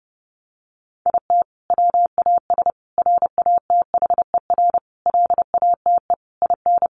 Morse code, heard as a keyed tone, about 30 wpm, 700 Hz. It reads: IT WAS RATHER LATE IN
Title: Sentences from Top 1400 Words 30wpm